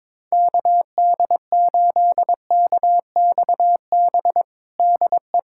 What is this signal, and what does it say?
Morse code, heard as a keyed tone, about 22 wpm, 700 Hz. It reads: KD8KXB DE
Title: USA Callsign DE Callsign 22wpm